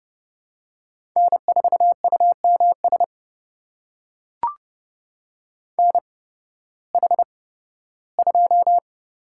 Morse code, heard as a keyed tone, about 30 wpm, spaced as for 16 wpm, 700 Hz. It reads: N4UMS NH2
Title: USA Call Signs - 5 Characters - 8x Character Spacing - 30wpm